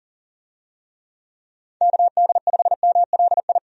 Morse code, heard as a keyed tone, about 40 wpm, 700 Hz. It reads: KD5MLI
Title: USA Call Signs - 6 Characters - 40wpm